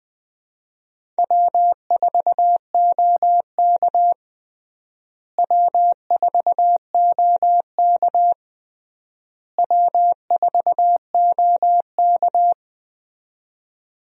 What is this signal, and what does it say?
Morse code, heard as a keyed tone, about 20 wpm, 700 Hz. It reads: W4OK W4OK W4OK
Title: USA Call Signs - 4 Characters - Repeated 3x - 20wpm